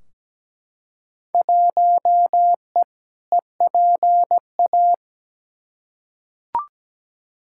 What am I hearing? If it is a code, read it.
Morse code, heard as a keyed tone, about 17 wpm, 700 Hz. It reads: 1E EPA
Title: Field Day Exchanges - 17wpm